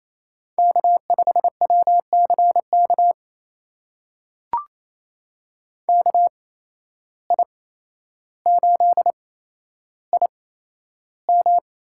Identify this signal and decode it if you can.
Morse code, heard as a keyed tone, about 28 wpm, spaced as for 12 wpm, 700 Hz. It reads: K5WCK KI8IM
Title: USA Call Signs - 5 Characters - 8x Character Spacing - 28wpm